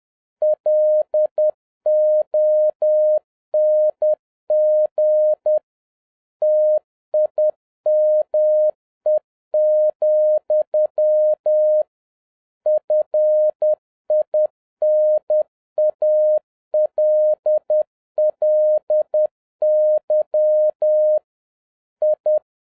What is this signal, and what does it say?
Morse code, heard as a keyed tone, about 10 wpm, 610 Hz. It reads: LONG TIME, FINALLY I
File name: 2011-07-mp3 / n41 EXCITING QRP CW OPERATION 10wpm